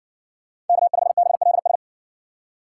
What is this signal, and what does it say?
Morse code, heard as a keyed tone, about 63 wpm, 700 Hz. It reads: K5BLS